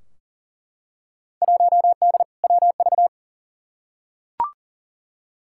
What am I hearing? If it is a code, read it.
Morse code, heard as a keyed tone, about 37 wpm, 700 Hz. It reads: 1D WV
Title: Field Day Exchanges - 40wpm